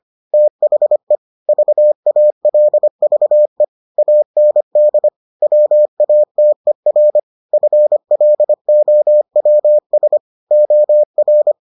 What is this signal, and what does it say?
Morse code, heard as a keyed tone, about 25 wpm, 600 Hz. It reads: THE VALVE AND WATER FLOWS OR